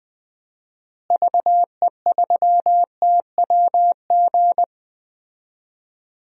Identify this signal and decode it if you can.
Morse code, heard as a keyed tone, about 20 wpm, 700 Hz. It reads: VE3TWG
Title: DXCC-100 Call Signs - 6 Characters - Repeated 3x - 20wpm